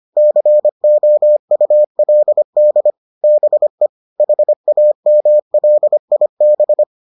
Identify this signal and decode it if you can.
Morse code, heard as a keyed tone, about 25 wpm, 600 Hz. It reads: COULD BE HAMLIB